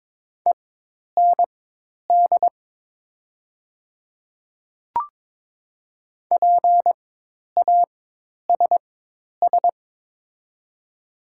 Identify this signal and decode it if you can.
Morse code, heard as a keyed tone, about 21 wpm, spaced as for 10 wpm, 700 Hz. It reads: END PASS
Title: 4x Character Spacing - Top 500 Words - 22wpm